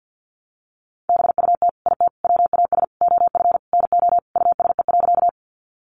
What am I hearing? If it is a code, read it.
Morse code, heard as a keyed tone, about 50 wpm, 700 Hz. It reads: BUT IT WAS OF NO USE.
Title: Sentences from Top 100 Words 50wpm